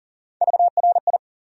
Morse code, heard as a keyed tone, about 40 wpm, 700 Hz. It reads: VRI
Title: USA Call Sign Postfixes - 3 Letters 40wpm